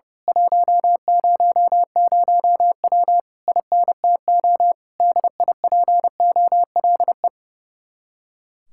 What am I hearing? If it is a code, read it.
Morse code, heard as a keyed tone, about 28 wpm, 700 Hz. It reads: 100W INTO DIPOLE
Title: Speed-Racing Common QSO Phrases Encore - 30wpm